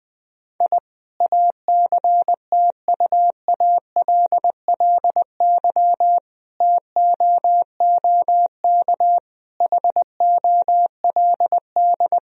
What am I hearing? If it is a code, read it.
Morse code, heard as a keyed tone, about 20 wpm, 700 Hz. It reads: I ACTUALLY TOOK HOLD
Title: Sentences from Top 1000 Words - Review Part 4 - 20wpm